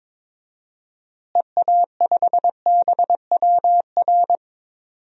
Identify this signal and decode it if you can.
Morse code, heard as a keyed tone, about 21 wpm, 700 Hz. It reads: EA5BWR
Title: DXCC-100 Call Signs - 6 Characters - Repeated 3x - 22wpm